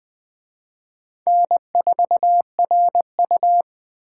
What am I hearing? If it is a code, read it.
Morse code, heard as a keyed tone, about 23 wpm, 700 Hz. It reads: N4RU
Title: Speed Racing - USA 1x2 Call Signs 20wpm